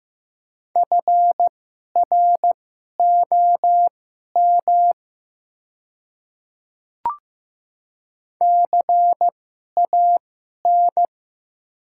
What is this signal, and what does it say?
Morse code, heard as a keyed tone, about 15 wpm, spaced as for 11 wpm, 700 Hz. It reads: FROM CAN
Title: 2x Character Spacing - Top 100 Words - 15wpm